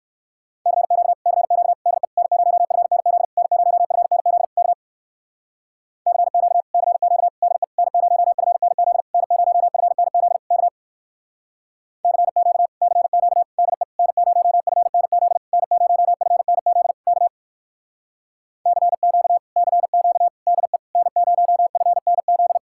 Morse code, heard as a keyed tone, about 59 wpm, 700 Hz. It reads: CQ CQ DE N0FNZ N0FNZ K CQ CQ DE N0FNZ N0FNZ K CQ CQ DE N0FNZ N0FNZ K CQ CQ DE N0FNZ